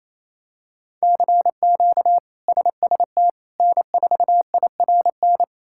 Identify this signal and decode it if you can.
Morse code, heard as a keyed tone, about 28 wpm, 700 Hz. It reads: CQ SST N4IRN
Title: K1USN SST QSO Practice - 28wpm